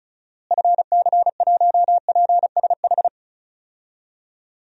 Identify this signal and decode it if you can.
Morse code, heard as a keyed tone, about 36 wpm, 700 Hz. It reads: FC1PSH
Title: DXCC-100 Call Signs - 6 Characters - Repeated 2x - 35wpm